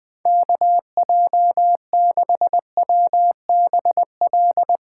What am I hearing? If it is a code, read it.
Morse code, heard as a keyed tone, about 20 wpm, 700 Hz. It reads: KJ6WBL
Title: Speed Racing - USA Call Signs 17wpm